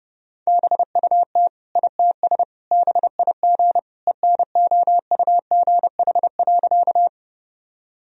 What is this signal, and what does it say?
Morse code, heard as a keyed tone, about 30 wpm, 700 Hz. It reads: BUT ITS BIG ENOUGH.
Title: Sentences from Top 1000 Words - Review Part 1 - 30wpm